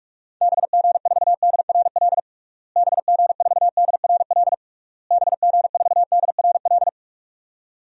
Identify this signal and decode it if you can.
Morse code, heard as a keyed tone, about 46 wpm, 700 Hz. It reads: BG4DRL BG4DRL BG4DRL
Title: DXCC-100 Call Signs - 6 Characters - Repeated 3x - 45wpm